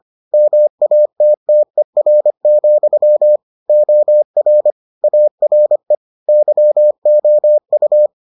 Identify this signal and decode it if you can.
Morse code, heard as a keyed tone, about 25 wpm, 600 Hz. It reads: MATTER, OR ARE YOU